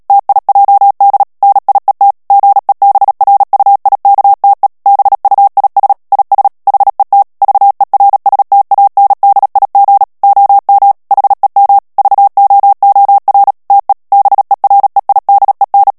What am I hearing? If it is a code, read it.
Morse code, heard as a keyed tone, about 37 wpm, 800 Hz. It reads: TIJD NIET GEBRUIKTE BUIS IS HET VERSTANDIG OM HEM VOOR TE BEREIDEN
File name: morse mp3 / HOLLANDAIS texte clair 2 vitesse 35 mots minute